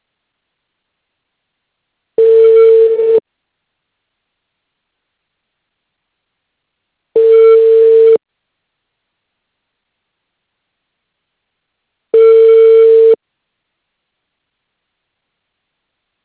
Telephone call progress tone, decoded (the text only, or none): ringback tone